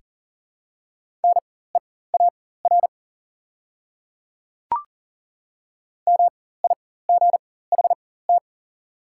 Morse code, heard as a keyed tone, about 37 wpm, spaced as for 18 wpm, 700 Hz. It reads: NEAR MIGHT
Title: 4x Character Spacing - Top 500 Words - 40wpm